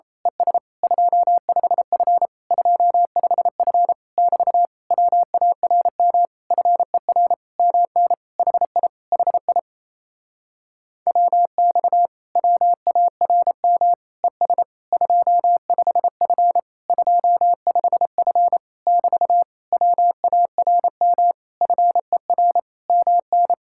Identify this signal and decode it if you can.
Morse code, heard as a keyed tone, about 28 wpm, 700 Hz. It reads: ES 25F 25F <BT> WARM FER MN HI HI WX WARM ES 25F 25F <BT> WARM FER MN
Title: Speed-Racing Common QSO Phrases Encore - 28wpm